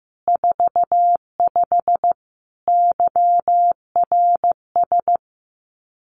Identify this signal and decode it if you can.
Morse code, heard as a keyed tone, about 15 wpm, 700 Hz. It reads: 45 YRS
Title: Common QSO Phrases Encore 15wpm